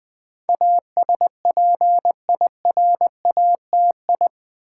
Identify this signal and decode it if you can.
Morse code, heard as a keyed tone, about 20 wpm, 700 Hz. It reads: ASPIRATI
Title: Build A Word - 4 Letters At A Time - 10-Letter Words - 20wpm